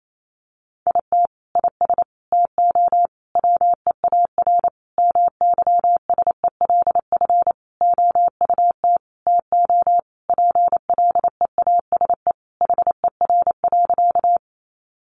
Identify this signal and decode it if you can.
Morse code, heard as a keyed tone, about 28 wpm, 700 Hz. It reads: IT IS TO WEAR MYSELF OUT TO PLEASE HER.
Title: Sentences from Top 1500 Words 28wpm